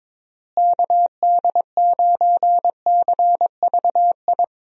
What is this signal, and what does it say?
Morse code, heard as a keyed tone, about 22 wpm, 700 Hz. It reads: KD9CVI
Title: USA Call Signs - 6 Characters - Repeated 3x - 22wpm